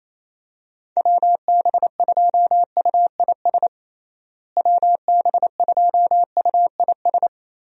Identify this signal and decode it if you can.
Morse code, heard as a keyed tone, about 28 wpm, 700 Hz. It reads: WB2UIS WB2UIS
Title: USA Call Signs - 6 Characters - Repeated 2x - 28wpm